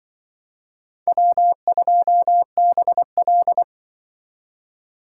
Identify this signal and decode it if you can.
Morse code, heard as a keyed tone, about 25 wpm, 700 Hz. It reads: W2BL
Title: Speed Racing - USA 1x2 Call Signs 20wpm